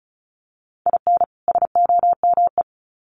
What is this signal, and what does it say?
Morse code, heard as a keyed tone, about 34 wpm, 700 Hz. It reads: IN SOME
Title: Sets of 2 Words 35wpm